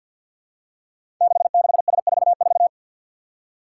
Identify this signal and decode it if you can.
Morse code, heard as a keyed tone, about 50 wpm, 700 Hz. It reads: 66S44
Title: Five Challenging Letter-Number Combinations 50wpm